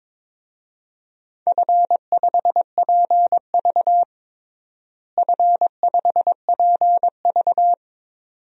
Morse code, heard as a keyed tone, about 22 wpm, 700 Hz. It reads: F5PV F5PV
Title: DXCC-100 Call Signs - 4 Characters - Repeated 2x - 22wpm